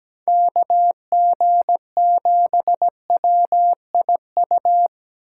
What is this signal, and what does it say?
Morse code, heard as a keyed tone, about 17 wpm, 700 Hz. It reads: KG7WIU
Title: USA Call Signs - 6 Characters - 3x Character Spacing - 17wpm